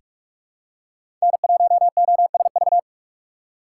Morse code, heard as a keyed tone, about 42 wpm, 700 Hz. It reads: N1OSV
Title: USA Call Signs - 4x Character Spacing - 45wpm